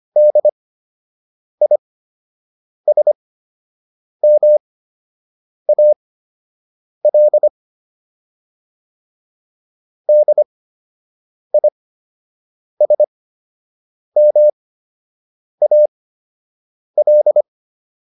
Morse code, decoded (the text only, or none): DISMAL DISMAL